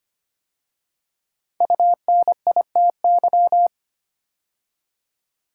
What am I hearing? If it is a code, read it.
Morse code, heard as a keyed tone, about 25 wpm, 700 Hz. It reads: UNITY
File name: 5 Letter Words - First 3 Letters Given - 25wpm